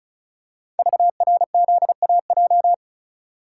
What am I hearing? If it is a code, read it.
Morse code, heard as a keyed tone, about 36 wpm, 700 Hz. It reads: VRZAJ
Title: Code Groups - 5 Letters - 35wpm